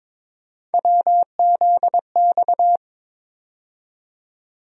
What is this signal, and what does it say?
Morse code, heard as a keyed tone, about 22 wpm, 700 Hz. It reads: WZX
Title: USA Call Sign Postfixes - 3 Letters 22wpm